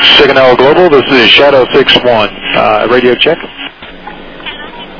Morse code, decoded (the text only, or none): none